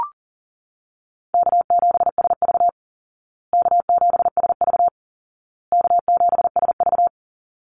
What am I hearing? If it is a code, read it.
Morse code, heard as a keyed tone, about 40 wpm, 700 Hz. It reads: K7SV K7SV K7SV
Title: Easy Call Signs Repeated 3x - 40wpm